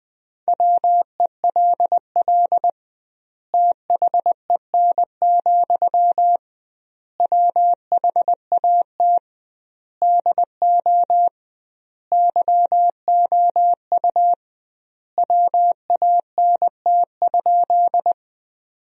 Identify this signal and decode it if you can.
Morse code, heard as a keyed tone, about 20 wpm, 700 Hz. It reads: WELL THEN, WHAT DO YOU WANT?
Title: 2x Word-Spacing - Sentences from Top 200 Words - 20wpm